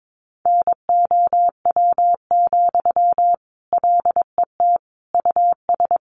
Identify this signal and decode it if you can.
Morse code, heard as a keyed tone, about 22 wpm, 700 Hz. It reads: NOW, LET US
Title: Sentences from Top 800 Words 22wpm